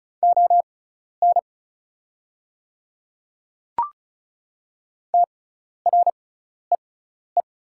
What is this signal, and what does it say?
Morse code, heard as a keyed tone, about 34 wpm, spaced as for 12 wpm, 690 Hz. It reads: ON TREE
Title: 6x Character Spacing - Top 300 Words - 35wpm